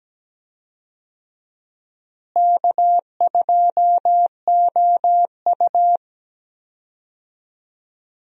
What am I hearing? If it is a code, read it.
Morse code, heard as a keyed tone, about 17 wpm, 700 Hz. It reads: K2OU
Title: USA Call Signs - 4 Characters - Repeated 2x - 17wpm